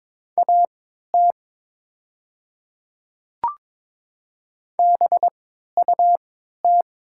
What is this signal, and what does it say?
Morse code, heard as a keyed tone, about 22 wpm, spaced as for 12 wpm, 690 Hz. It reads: AT BUT
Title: 3x Character Spacing - Top 100 Words - 22wpm